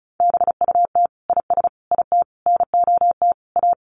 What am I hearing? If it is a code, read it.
Morse code, heard as a keyed tone, about 34 wpm, 700 Hz. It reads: BUT IS IT NOT A
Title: Sentences from Top 1500 Words 35wpm